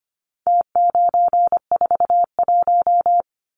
Morse code, heard as a keyed tone, about 25 wpm, 700 Hz. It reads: T941
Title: Numbers - Contest Style - 4-Digit 25wpm